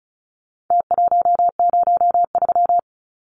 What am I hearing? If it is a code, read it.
Morse code, heard as a keyed tone, about 34 wpm, 700 Hz. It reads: T103
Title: Numbers - Contest Style - 4-Digit 35wpm